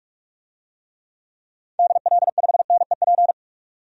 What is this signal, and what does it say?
Morse code, heard as a keyed tone, about 47 wpm, 700 Hz. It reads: DL5NEP
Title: DXCC-100 Call Signs - 6 Characters - 45wpm